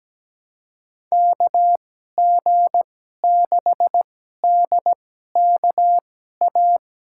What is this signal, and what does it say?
Morse code, heard as a keyed tone, about 17 wpm, spaced as for 12 wpm, 700 Hz. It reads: KG6DKA
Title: USA Call Signs - 6 Characters - 2x Character Spacing - 17wpm